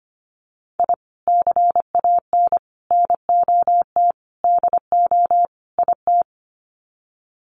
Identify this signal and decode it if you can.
Morse code, heard as a keyed tone, about 25 wpm, 700 Hz. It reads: I CAN NOT DO IT
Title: Sets of 5 Words Encore 25wpm